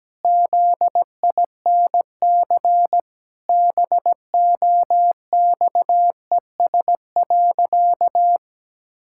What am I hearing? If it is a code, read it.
Morse code, heard as a keyed tone, about 17 wpm, 700 Hz. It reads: ZINC BOXES.
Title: Pangram Challenge v2 - 17wpm